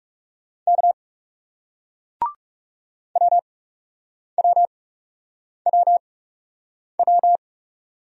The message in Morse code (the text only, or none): K WWWW